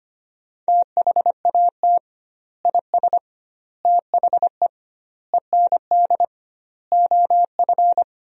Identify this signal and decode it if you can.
Morse code, heard as a keyed tone, about 25 wpm, 700 Hz. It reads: THAT IS THE END OF
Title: 2x Word-Spacing - Sentences from Top 200 Words - 25wpm